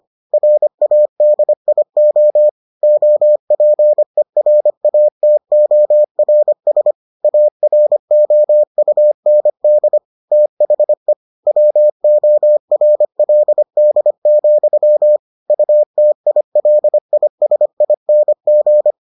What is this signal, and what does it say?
Morse code, heard as a keyed tone, about 25 wpm, 600 Hz. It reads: RADIO OPERATORS AROUND THE WORLD, UTILISING